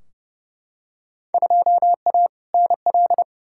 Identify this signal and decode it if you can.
Morse code, heard as a keyed tone, about 31 wpm, 700 Hz. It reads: 2A NL
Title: Field Day Exchanges Encore - 30wpm